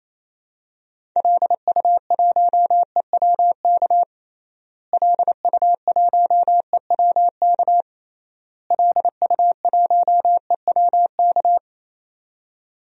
Morse code, heard as a keyed tone, about 27 wpm, 700 Hz. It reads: LU1EWK LU1EWK LU1EWK
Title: DXCC-100 Call Signs - 6 Characters - Repeated 3x - 28wpm